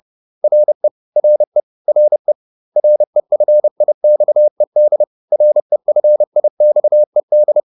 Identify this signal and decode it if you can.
Morse code, heard as a keyed tone, about 30 wpm, 600 Hz. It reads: RE RE RE REFIXED REFIXED